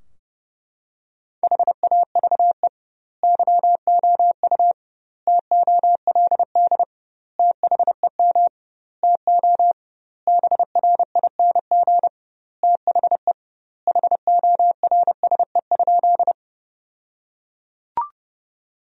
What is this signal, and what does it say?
Morse code, heard as a keyed tone, about 30 wpm, 700 Hz. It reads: HAVE YOU TOLD THEM TO BRING THE HORSE?
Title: 2x Word-Spacing - Sentences from Top 500 Words - 30wpm